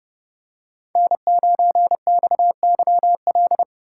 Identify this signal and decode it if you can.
Morse code, heard as a keyed tone, about 29 wpm, 700 Hz. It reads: N9XYL